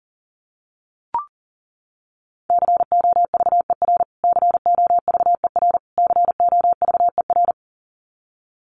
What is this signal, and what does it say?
Morse code, heard as a keyed tone, about 40 wpm, 700 Hz. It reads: COVER COVER COVER
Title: Top 500 Words Repeated 40wpm